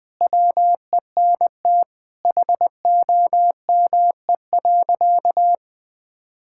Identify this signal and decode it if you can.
Morse code, heard as a keyed tone, about 19 wpm, 700 Hz. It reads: WENT HOME.
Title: Sentences with 4-Word Phrases - Spoken Beginning - 20wpm